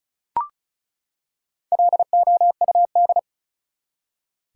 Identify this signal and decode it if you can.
Morse code, heard as a keyed tone, about 34 wpm, 700 Hz. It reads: LOUD
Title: Four Letter Words - 35wpm